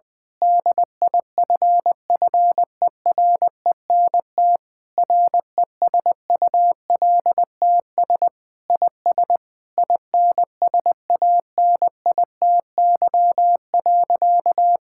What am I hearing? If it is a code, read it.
Morse code, heard as a keyed tone, about 20 wpm, 700 Hz. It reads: DIFFERENT RESULTS IS INSANITY.
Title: Sentences with 9-Word Phrases - 20wpm